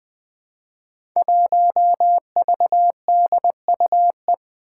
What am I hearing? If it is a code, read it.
Morse code, heard as a keyed tone, about 20 wpm, 700 Hz. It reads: 1VDUE
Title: Five Challenging Letter-Number Combinations 20wpm